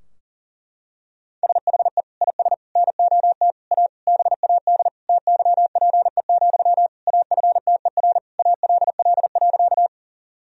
Morse code, heard as a keyed tone, about 40 wpm, 700 Hz. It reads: SHE IS NOT A BAD TYPE, AFTER ALL.
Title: Sentences from Top 1000 Words - Review Part 3 - 40wpm